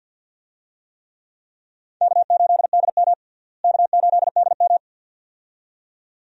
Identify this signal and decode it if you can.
Morse code, heard as a keyed tone, about 49 wpm, 700 Hz. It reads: K8DM K8DM
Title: USA Call Signs - 4 Characters - Repeated 2x - 50wpm